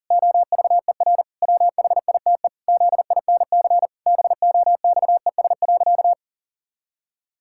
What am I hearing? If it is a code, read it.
Morse code, heard as a keyed tone, about 40 wpm, 700 Hz. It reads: OVER WHITE ZINC BOXES.
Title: Pangram Challenge v2 - 40wpm